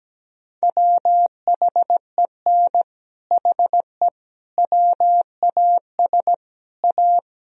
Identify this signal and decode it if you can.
Morse code, heard as a keyed tone, about 17 wpm, 700 Hz. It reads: WHEN HE WAS A
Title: Sentences from Top 1000 Words - Review Part 3 - 17wpm